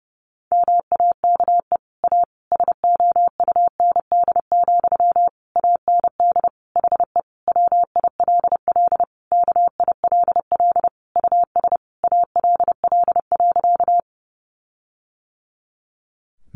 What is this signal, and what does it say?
Morse code, heard as a keyed tone, about 30 wpm, 700 Hz. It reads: MAKE A SOUND, AND HE WILL KILL US ALL.
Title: Sentences from Top 1400 Words 30wpm